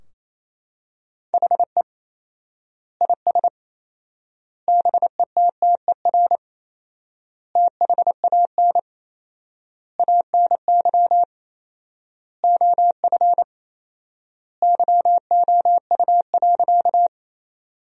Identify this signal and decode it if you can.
Morse code, heard as a keyed tone, about 28 wpm, 700 Hz. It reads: HE IS BETTER THAN ANY OF YOU.
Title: 4x Word-Spacing - Sentences from Top 400 Words - 28wpm